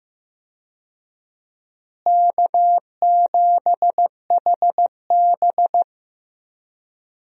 Morse code, heard as a keyed tone, about 15 wpm, 700 Hz. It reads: K7HB
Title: USA Call Signs - 4 Characters - Repeated 3x - 15wpm